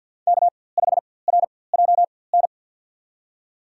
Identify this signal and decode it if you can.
Morse code, heard as a keyed tone, about 52 wpm, spaced as for 23 wpm, 700 Hz. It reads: K5RJN
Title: USA Call Signs - 5 Characters - 4x Character Spacing - 50wpm